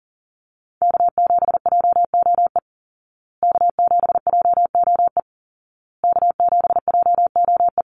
Morse code, heard as a keyed tone, about 38 wpm, 700 Hz. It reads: K7JOE K7JOE K7JOE
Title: Call Signs Repeated 2x - 40wpm